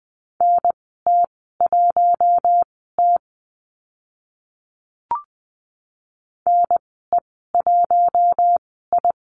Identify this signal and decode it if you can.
Morse code, heard as a keyed tone, about 20 wpm, spaced as for 14 wpm, 700 Hz. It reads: NT1T NE1I
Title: 042 Lesson 8g - Number 1 - Call Signs 20wpm